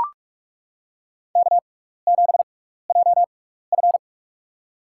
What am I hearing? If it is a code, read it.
Morse code, heard as a keyed tone, about 42 wpm, spaced as for 15 wpm, 700 Hz. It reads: K7JF
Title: USA Call Signs - 4 Characters - 6x Character Spacing - 45wpm